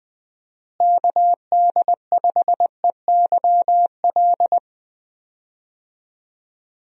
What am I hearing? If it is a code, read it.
Morse code, heard as a keyed tone, about 21 wpm, 700 Hz. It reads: KD5EYL
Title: USA Call Signs - 6 Characters - Repeated 3x - 20wpm